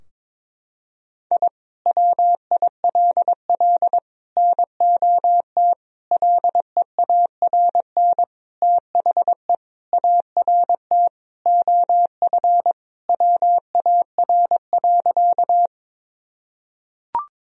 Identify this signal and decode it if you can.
Morse code, heard as a keyed tone, about 22 wpm, 700 Hz. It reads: I WILL NOT LEARN THE ART OF WAR.
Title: Sentences from Top 1000 Words - Review Part 3 - 22wpm